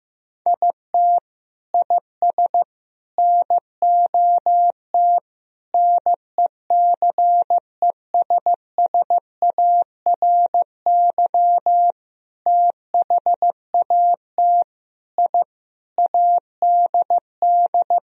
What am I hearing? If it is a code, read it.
Morse code, heard as a keyed tone, about 15 wpm, 700 Hz. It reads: IT IS NOT NECESSARY THAT I ADD
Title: Sentences from Top 1000 Words - Review Part 1 - 15wpm